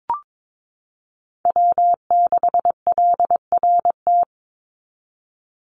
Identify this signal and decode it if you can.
Morse code, heard as a keyed tone, about 22 wpm, 700 Hz. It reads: W6LRT
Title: USA Call Signs 22wpm